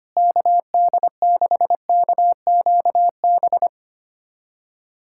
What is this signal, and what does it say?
Morse code, heard as a keyed tone, about 25 wpm, 700 Hz. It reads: KD6KQB
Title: USA Callsign DE Callsign 25wpm